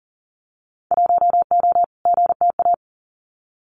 Morse code, heard as a keyed tone, about 42 wpm, 700 Hz. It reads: 1O GTA
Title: Winter Field Day Exchanges 40wpm